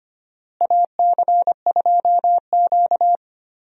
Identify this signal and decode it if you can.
Morse code, heard as a keyed tone, about 25 wpm, 700 Hz. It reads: AC2Q
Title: USA Call Signs - 4 Characters - Repeated 2x - 25wpm